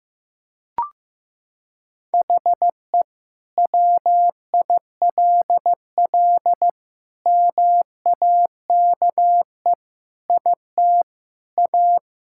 Morse code, heard as a keyed tone, about 15 wpm, 700 Hz. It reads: HE WILL MAKE IT A
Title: Sentences from Top 1000 Words - Review Part 5 - 15wpm